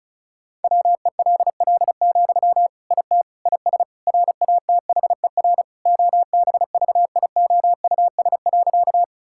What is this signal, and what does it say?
Morse code, heard as a keyed tone, about 36 wpm, 700 Hz. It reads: WELL, IT IS RATHER OBVIOUS.
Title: Sentences from Top 1600 Words 35wpm